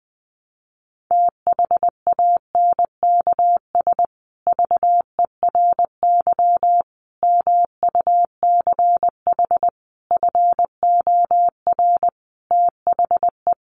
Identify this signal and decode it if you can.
Morse code, heard as a keyed tone, about 20 wpm, 700 Hz. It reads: THANKS VERY MUCH FOR THE
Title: Sets of 5 Words Encore2 20wpm